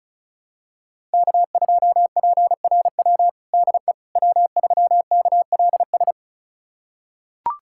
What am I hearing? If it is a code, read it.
Morse code, heard as a keyed tone, about 36 wpm, 700 Hz. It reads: K2PRW DE W3KLS
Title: USA Callsign DE Callsign - 5 Characters - 35wpm